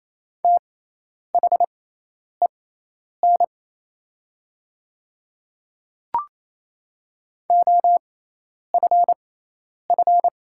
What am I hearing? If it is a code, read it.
Morse code, heard as a keyed tone, about 28 wpm, spaced as for 10 wpm, 700 Hz. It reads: THEN OFF
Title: 6x Character Spacing - Top 200 Words - 28wpm